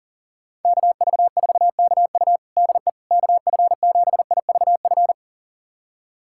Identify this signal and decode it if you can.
Morse code, heard as a keyed tone, about 40 wpm, 700 Hz. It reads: KV4KU DE KF7IVF
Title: USA Callsign DE Callsign 40wpm